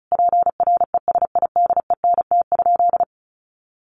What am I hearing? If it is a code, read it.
Morse code, heard as a keyed tone, about 34 wpm, 700 Hz. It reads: PRESIDENT?
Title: Sentences from Top 1300 Words 35wpm